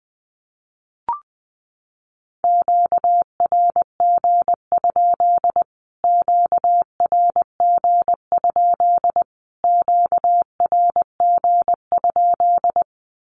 Morse code, heard as a keyed tone, about 20 wpm, 700 Hz. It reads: QRG? QRG? QRG?
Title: Q-Codes Repeated 3x 20wpm